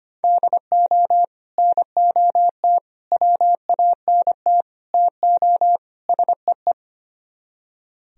Morse code, heard as a keyed tone, about 33 wpm, 700 Hz. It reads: DO NOT WANT TO SEE